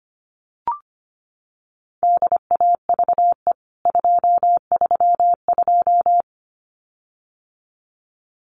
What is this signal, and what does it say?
Morse code, heard as a keyed tone, about 25 wpm, 690 Hz. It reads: DAVE 232
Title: CWT Exchange 25wpm